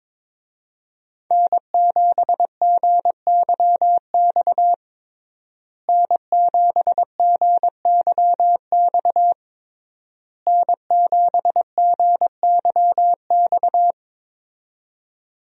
Morse code, heard as a keyed tone, about 21 wpm, 700 Hz. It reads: N7GYX N7GYX N7GYX
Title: USA Call Signs - 5 Characters - Repeated 3x - 22wpm